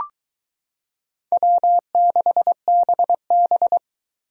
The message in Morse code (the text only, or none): W6BB